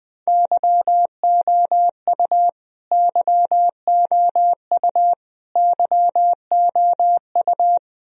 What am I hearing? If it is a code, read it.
Morse code, heard as a keyed tone, about 20 wpm, 700 Hz. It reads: YOU YOU YOU
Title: Top 100 Words Repeated 20wpm